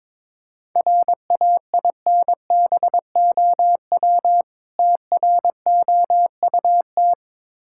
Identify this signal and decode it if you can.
Morse code, heard as a keyed tone, about 23 wpm, 690 Hz. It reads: RAINBOW TROUT
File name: Difficult Words with Hints - 22wpm